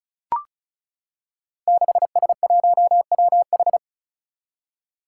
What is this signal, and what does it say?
Morse code, heard as a keyed tone, about 34 wpm, 700 Hz. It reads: 6S1WH
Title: Five Challenging Letter-Number Combinations 35wpm